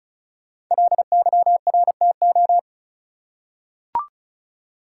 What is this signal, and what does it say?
Morse code, heard as a keyed tone, about 35 wpm, 700 Hz. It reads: LYRTO
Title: Code Groups - 5 Letters - 35wpm